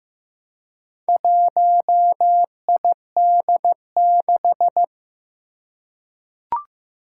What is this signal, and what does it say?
Morse code, heard as a keyed tone, about 15 wpm, 700 Hz. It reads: 1ID6
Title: Four Challenging Letter-Number Combinations 15wpm